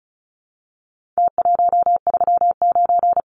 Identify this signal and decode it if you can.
Morse code, heard as a keyed tone, about 34 wpm, 700 Hz. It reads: T139
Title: Numbers - Contest Style - 4-Digit 35wpm